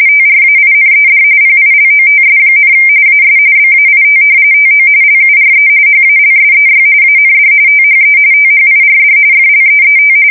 RTTY, about 45.45 baud. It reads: TRAINED PILOTS, AS BAIT TO LURE HALSEY AND THE THIRD FLEET
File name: fsk170 / battleshipnj